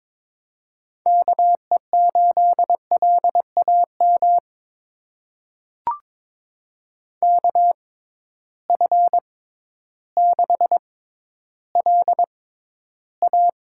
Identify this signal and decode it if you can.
Morse code, heard as a keyed tone, about 22 wpm, spaced as for 12 wpm, 700 Hz. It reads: KE8LAM KF6LA
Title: USA Call Signs - 6 Characters - 6x Character Spacing - 22wpm